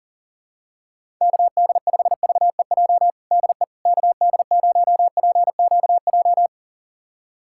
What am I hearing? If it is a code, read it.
Morse code, heard as a keyed tone, about 40 wpm, 700 Hz. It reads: KD5VEJ DE KD0PQJ
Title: USA Callsign DE Callsign - 6 Characters - 40wpm